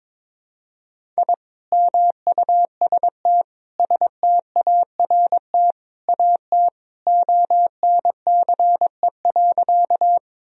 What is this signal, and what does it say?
Morse code, heard as a keyed tone, about 21 wpm, 700 Hz. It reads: I MUST START AT ONCE.
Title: Sentences from Top 1000 Words - Review Part 1 - 22wpm